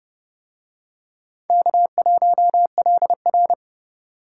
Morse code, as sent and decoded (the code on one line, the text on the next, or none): -.- .---- .-.. .-.
K1LR